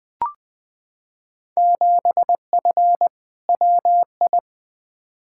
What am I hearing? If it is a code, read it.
Morse code, heard as a keyed tone, about 20 wpm, 700 Hz. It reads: 7F WI
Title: Field Day Exchanges Encore - 20wpm